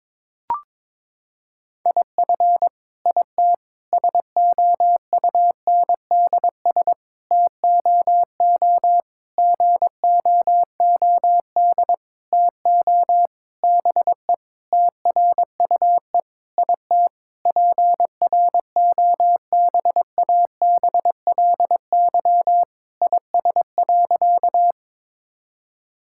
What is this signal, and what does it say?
Morse code, heard as a keyed tone, about 22 wpm, 700 Hz. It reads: IF IT SOUNDS TOO GOOD TO BE TRUE IT PROBABLY IS.
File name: Sentences with 9-Word Phrases - 22wpm